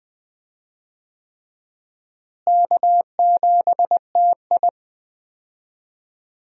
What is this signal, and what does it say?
Morse code, heard as a keyed tone, about 20 wpm, 700 Hz. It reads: K7TI
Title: USA Call Signs - 4 Characters - Repeated 3x - 20wpm